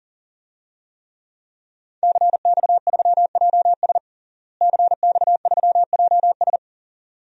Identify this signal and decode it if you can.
Morse code, heard as a keyed tone, about 38 wpm, 700 Hz. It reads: CX3JS CX3JS
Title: DXCC-100 Call Signs - 5 Characters - Repeated 2x - 40wpm